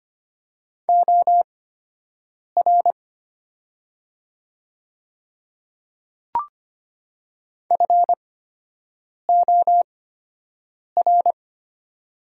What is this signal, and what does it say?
Morse code, heard as a keyed tone, about 25 wpm, spaced as for 7 wpm, 700 Hz. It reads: OR FOR
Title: 8x Character Spacing - Top 100 Words - 25wpm